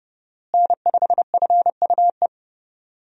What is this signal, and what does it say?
Morse code, heard as a keyed tone, about 30 wpm, 700 Hz. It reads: N5FUE
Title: USA Call Signs - 5 Characters - Repeated 2x - 30wpm